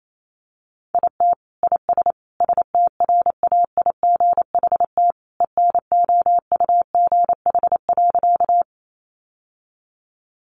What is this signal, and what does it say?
Morse code, heard as a keyed tone, about 28 wpm, 700 Hz. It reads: IT IS STRAIGHT ENOUGH.
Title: Sentences from Top 1000 Words 28wpm